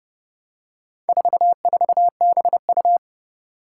Morse code, heard as a keyed tone, about 31 wpm, 700 Hz. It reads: 44BU
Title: Four Challenging Letter-Number Combinations 30wpm